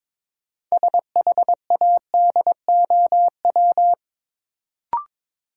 Morse code, heard as a keyed tone, about 22 wpm, 700 Hz. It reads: SHADOW
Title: Sets of 5 Words Encore - New Word Drill - 22wpm